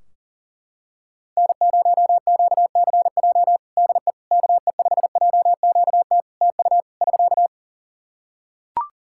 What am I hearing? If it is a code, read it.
Morse code, heard as a keyed tone, about 40 wpm, 700 Hz. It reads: N0QCJ DE KE5JQT TU <SK>
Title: Practice QSO - Exchange 4 - Ending - 40wpm